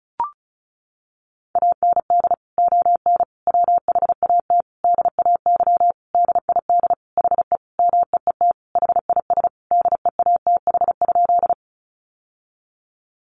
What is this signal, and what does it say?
Morse code, heard as a keyed tone, about 34 wpm, 700 Hz. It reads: AND ON WHAT DAY DID HE MEET HIS DEATH?
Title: Sentences from Top 1100 Words 35wpm